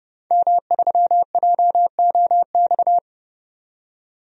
Morse code, heard as a keyed tone, about 31 wpm, 700 Hz. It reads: M3JOX
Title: DXCC-100 Call Signs - 5 Characters - 30wpm